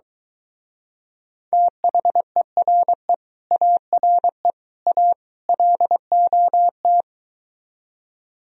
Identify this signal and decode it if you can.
Morse code, heard as a keyed tone, about 23 wpm, 700 Hz. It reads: THERE ARE A LOT